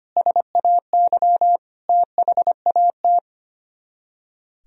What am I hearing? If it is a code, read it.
Morse code, heard as a keyed tone, about 32 wpm, 700 Hz. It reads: SAY THAT